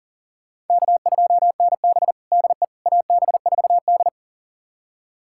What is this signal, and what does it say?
Morse code, heard as a keyed tone, about 40 wpm, 700 Hz. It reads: K2NB DE AB4D
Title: USA Callsign DE Callsign - 4 Characters - 40wpm